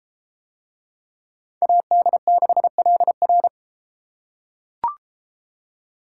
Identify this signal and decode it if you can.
Morse code, heard as a keyed tone, about 30 wpm, 700 Hz. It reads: AD6LR